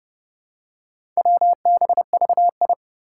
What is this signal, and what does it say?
Morse code, heard as a keyed tone, about 30 wpm, 700 Hz. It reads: WBVI
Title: Four Challenging Letter-Number Combinations 30wpm